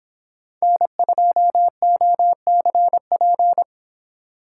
Morse code, heard as a keyed tone, about 29 wpm, 700 Hz. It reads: N2OCP